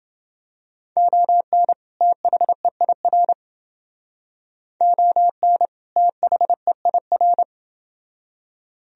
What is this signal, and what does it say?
Morse code, heard as a keyed tone, about 30 wpm, 700 Hz. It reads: ON THEIR ON THEIR